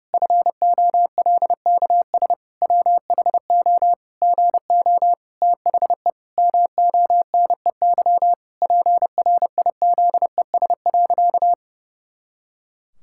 Morse code, heard as a keyed tone, about 32 wpm, 700 Hz. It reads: FOLKS WHO GO THE MONEY PRIZES.
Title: Pangram Challenge v2 - 30wpm